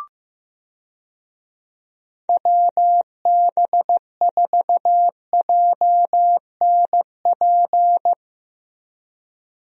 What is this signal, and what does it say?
Morse code, heard as a keyed tone, about 15 wpm, 700 Hz. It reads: WB4JNP
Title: USA Call Signs - 6 Characters - Repeated 3x - 15wpm